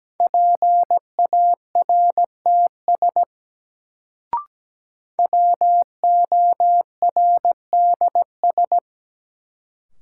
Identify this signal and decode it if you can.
Morse code, heard as a keyed tone, about 17 wpm, 700 Hz. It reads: PARTS WORDS
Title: Sets of 3 Words - New Word Drill - 17wpm